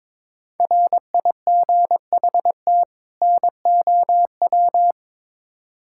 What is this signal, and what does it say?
Morse code, heard as a keyed tone, about 22 wpm, 700 Hz. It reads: RIGHT NOW
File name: Build A Phrase - 2-Word Phrases - 22wpm